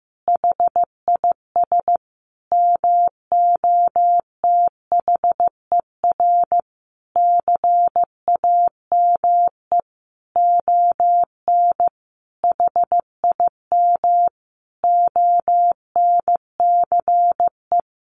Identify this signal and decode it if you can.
Morse code, heard as a keyed tone, about 15 wpm, 700 Hz. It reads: HIS MOTHER CAME ON HIM ONCE
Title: Sentences from Top 1100 Words 15wpm